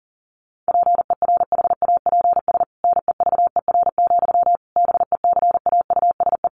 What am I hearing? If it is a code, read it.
Morse code, heard as a keyed tone, about 40 wpm, 700 Hz. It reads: PERHAPS NEVER, BECAUSE
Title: Sentences from Top 1200 Words 40wpm